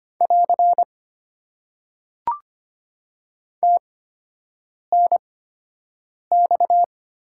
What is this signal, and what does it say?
Morse code, heard as a keyed tone, about 25 wpm, spaced as for 7 wpm, 700 Hz. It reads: <AR> TNX
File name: Top 90 QSO Elements - 8x Character Spacing - 25wpm